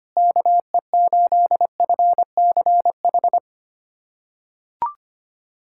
Morse code, heard as a keyed tone, about 25 wpm, 700 Hz. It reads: KE8FCH
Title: USA Callsign DE Callsign - 6 Characters - 25wpm